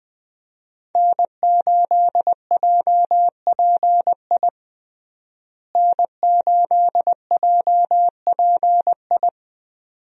Morse code, heard as a keyed tone, about 20 wpm, 700 Hz. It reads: N8JPI N8JPI
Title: USA Call Signs - 5 Characters - Repeated 2x - 20wpm